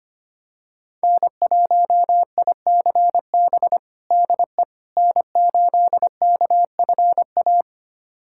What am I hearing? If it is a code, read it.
Morse code, heard as a keyed tone, about 25 wpm, 700 Hz. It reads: N1ICB DE N8KFA
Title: USA Callsign DE Callsign - 5 Characters - 25wpm